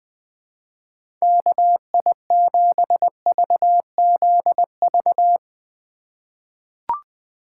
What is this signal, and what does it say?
Morse code, heard as a keyed tone, about 20 wpm, 700 Hz. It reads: KI7VZV
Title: USA Call Signs - 2x Character Spacing - 20wpm